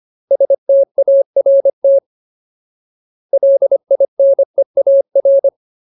Morse code, heard as a keyed tone, about 25 wpm, 550 Hz. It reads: START LINEAR
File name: Ultimate Word List / 25WPM Ultimate